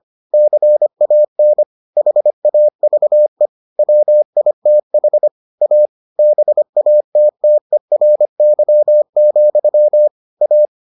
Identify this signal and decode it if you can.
Morse code, heard as a keyed tone, about 25 wpm, 600 Hz. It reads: CAN HAVE WITH A BATTERY, A